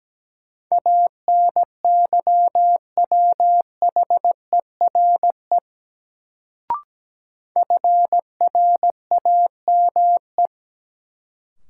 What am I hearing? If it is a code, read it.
Morse code, heard as a keyed tone, about 17 wpm, 700 Hz. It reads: ANYWHERE FRAME
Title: Sentences from Top 1400 Words - New Word Drill Alt - 17wpm